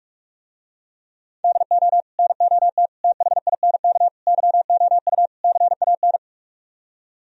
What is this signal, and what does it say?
Morse code, heard as a keyed tone, about 46 wpm, 700 Hz. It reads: DO NOT THINK YOU CAN